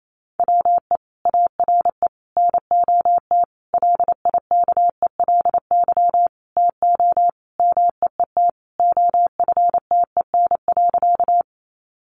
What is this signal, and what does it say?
Morse code, heard as a keyed tone, about 28 wpm, 700 Hz. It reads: WE ARE NOT LIKELY TO MEET OFTEN.
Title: Sentences from Top 1100 Words 28wpm